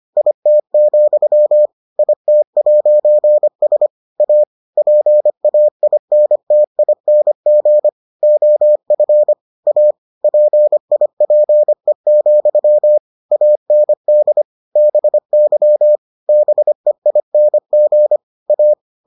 Morse code, heard as a keyed tone, about 25 wpm, 600 Hz. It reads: IT, IT'S A PAINTING OF A PIPE, AND BY BEING A